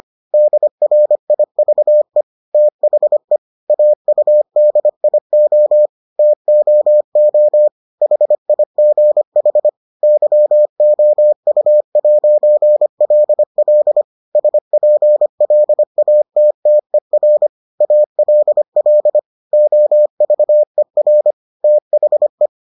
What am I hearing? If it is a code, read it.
Morse code, heard as a keyed tone, about 25 wpm, 600 Hz. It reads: DRIVE THE AUDIO TOO HIGH YOU'LL SPLATTER ALL OVER THE